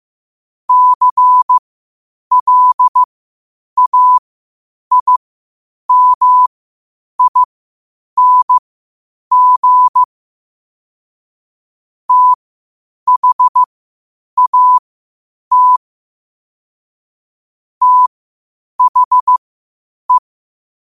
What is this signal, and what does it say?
Morse code, heard as a keyed tone, about 15 wpm, spaced as for 8 wpm, 1000 Hz. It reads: CLAIMING THAT THE
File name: qotdfiles / 8wpm